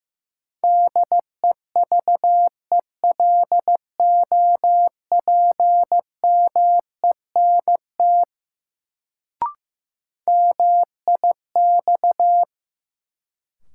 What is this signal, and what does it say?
Morse code, heard as a keyed tone, about 15 wpm, 700 Hz. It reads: DEVELOPMENT MIX
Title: Sentences from Top 1000 Words - Review Part 4 - New Word Drill - 15wpm